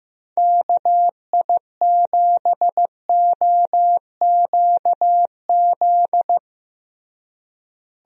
Morse code, heard as a keyed tone, about 15 wpm, 700 Hz. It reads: KI7OQZ
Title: USA Call Signs - 6 Characters - Repeated 3x - 15wpm